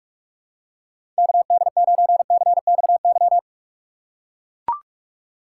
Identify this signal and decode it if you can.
Morse code, heard as a keyed tone, about 43 wpm, 700 Hz. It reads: KD9CXY
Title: USA Call Signs - 6 Characters - 2x Character Spacing - 45wpm